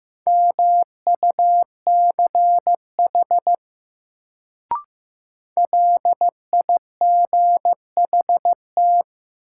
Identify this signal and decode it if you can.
Morse code, heard as a keyed tone, about 15 wpm, 700 Hz. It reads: MUCH LIGHT
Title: Sentences from Top 200 Words - New Word Drill - 15wpm